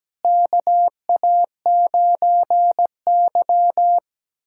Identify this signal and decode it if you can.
Morse code, heard as a keyed tone, about 17 wpm, 700 Hz. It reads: KA9Y
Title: USA Callsign DE Callsign - 4 Characters - 17wpm